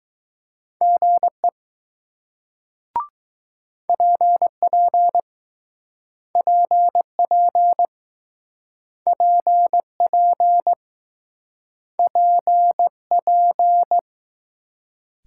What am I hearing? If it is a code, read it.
Morse code, heard as a keyed tone, about 19 wpm, 700 Hz. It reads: GE PP PP PP PP